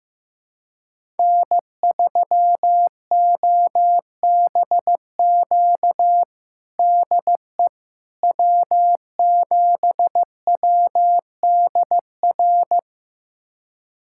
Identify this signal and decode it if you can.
Morse code, heard as a keyed tone, about 16 wpm, 700 Hz. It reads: N3OBQ DE W7WDR
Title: USA Callsign DE Callsign - 5 Characters - 15wpm